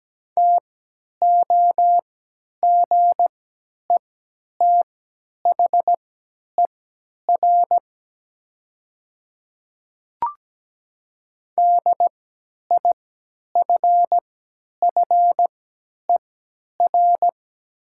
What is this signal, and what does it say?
Morse code, heard as a keyed tone, about 17 wpm, spaced as for 10 wpm, 700 Hz. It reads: TOGETHER DIFFER
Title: 3x Character Spacing - Top 300 Words - 17wpm